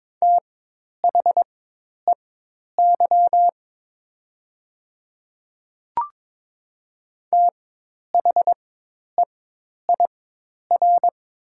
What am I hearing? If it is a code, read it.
Morse code, heard as a keyed tone, about 22 wpm, spaced as for 10 wpm, 700 Hz. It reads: THEY THEIR
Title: 4x Character Spacing - Top 100 Words - 22wpm